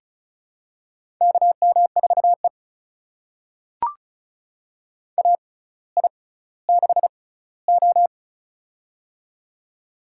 Morse code, heard as a keyed tone, about 36 wpm, spaced as for 18 wpm, 700 Hz. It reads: KM4E AI6O
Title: USA Call Signs - 4 Characters - 6x Character Spacing - 35wpm